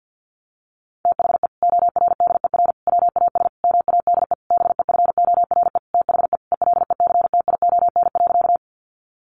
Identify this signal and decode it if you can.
Morse code, heard as a keyed tone, about 50 wpm, 700 Hz. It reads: THE ORDER WAS MADE BEFORE THE ELECTION.
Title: Sentences from Top 1500 Words 50wpm